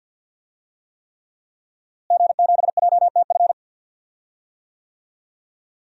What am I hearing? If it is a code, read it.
Morse code, heard as a keyed tone, about 52 wpm, 700 Hz. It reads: G7JTF
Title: DXCC-100 Call Signs - 5 Characters - Repeated 3x - 50wpm